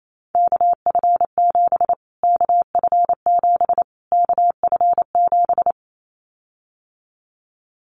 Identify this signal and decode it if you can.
Morse code, heard as a keyed tone, about 28 wpm, 700 Hz. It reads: KF7 KF7 KF7
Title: Callsign Prefixes for Top 25 DXCC - Repeated 3x - 28wpm